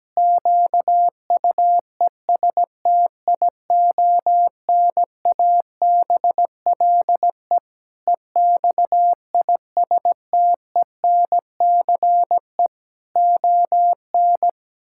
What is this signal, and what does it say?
Morse code, heard as a keyed tone, about 17 wpm, 700 Hz. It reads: QUESTIONABLE EXISTENCE ON
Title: Pangram Challenge v2 - 17wpm